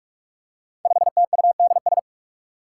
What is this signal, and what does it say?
Morse code, heard as a keyed tone, about 44 wpm, 700 Hz. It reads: 5TUDS
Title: Five Challenging Letter-Number Combinations 45wpm